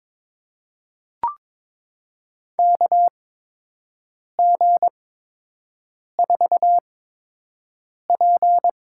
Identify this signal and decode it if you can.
Morse code, heard as a keyed tone, about 22 wpm, spaced as for 6 wpm, 700 Hz. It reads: KG4P
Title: USA Call Signs - 6 Characters - 8x Character Spacing - 22wpm